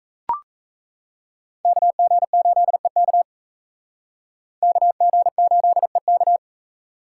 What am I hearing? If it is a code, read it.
Morse code, heard as a keyed tone, about 36 wpm, 700 Hz. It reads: KG8EK KG8EK